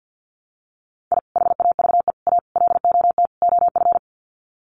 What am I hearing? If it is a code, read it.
Morse code, heard as a keyed tone, about 50 wpm, 700 Hz. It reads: I HAVE A LOT OF
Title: Sets of 5 Words 50wpm